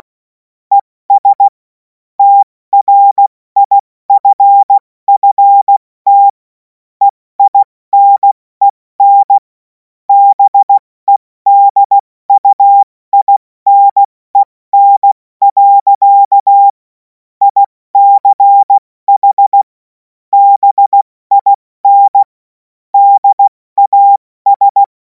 Morse code, heard as a keyed tone, about 16 wpm, 800 Hz. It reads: ES TRIFFT EINEN BEDUINEN. ICH BIN DAS
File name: Morsekurs / Witze / Witz 03. März 2026 70 F80